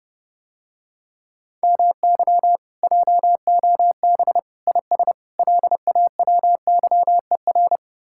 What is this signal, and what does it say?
Morse code, heard as a keyed tone, about 30 wpm, 700 Hz. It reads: MY JOB IS LAWYER